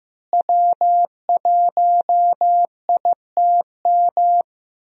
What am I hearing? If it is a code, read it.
Morse code, heard as a keyed tone, about 15 wpm, 700 Hz. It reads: W1ITM
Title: USA Call Signs - 5 Characters - Repeated 2x - 15wpm